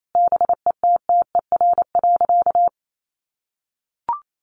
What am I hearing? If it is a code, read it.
Morse code, heard as a keyed tone, about 28 wpm, 700 Hz. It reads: BETTER.
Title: Sentences from Top 500 Words 28wpm